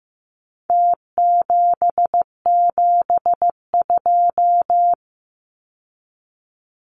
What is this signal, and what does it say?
Morse code, heard as a keyed tone, about 15 wpm, 700 Hz. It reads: T772
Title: Numbers - Contest Style - 4-Digit 15wpm